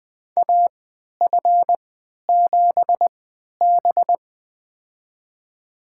Morse code, heard as a keyed tone, about 20 wpm, spaced as for 11 wpm, 700 Hz. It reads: AF7B
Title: USA Call Signs - 4 Characters - 3x Character Spacing - 20wpm